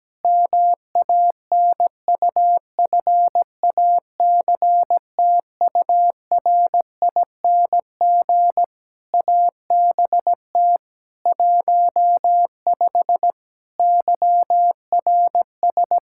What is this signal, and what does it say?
Morse code, heard as a keyed tone, about 17 wpm, 700 Hz. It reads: MANUFACTURING ABT 15 YRS
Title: Practice QSO - Exchange 3 - More Info - 17wpm